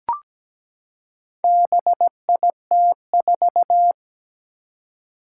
Morse code, heard as a keyed tone, about 17 wpm, 700 Hz. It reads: BIT4
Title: Four Challenging Letter-Number Combinations 17wpm